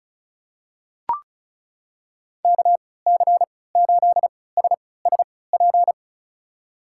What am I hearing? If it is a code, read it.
Morse code, heard as a keyed tone, about 36 wpm, spaced as for 20 wpm, 700 Hz. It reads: KC8SSP
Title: USA Call Signs - 3x Character Spacing - 35wpm